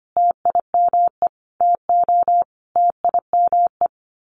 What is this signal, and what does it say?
Morse code, heard as a keyed tone, about 25 wpm, 700 Hz. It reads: TIME TO TIME
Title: Sets of 4 Words 25wpm